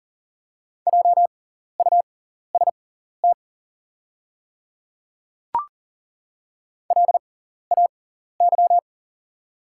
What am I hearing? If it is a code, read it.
Morse code, heard as a keyed tone, about 40 wpm, spaced as for 14 wpm, 700 Hz. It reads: JUST LAY
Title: 6x Character Spacing - Top 500 Words - 40wpm